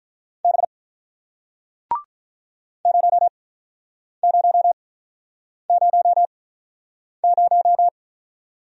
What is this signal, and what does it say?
Morse code, heard as a keyed tone, about 46 wpm, spaced as for 9 wpm, 700 Hz. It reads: B 0000